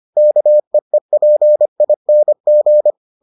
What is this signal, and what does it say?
Morse code, heard as a keyed tone, about 25 wpm, 600 Hz. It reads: KEEPING